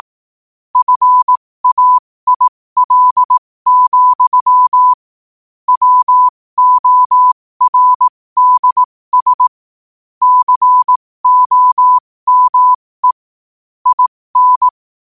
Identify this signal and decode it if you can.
Morse code, heard as a keyed tone, about 18 wpm, spaced as for 15 wpm, 1000 Hz. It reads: FAIL, WORDS COME IN
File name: qotdfiles / 15wpm